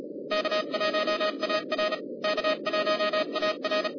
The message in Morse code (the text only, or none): K1AR K1AR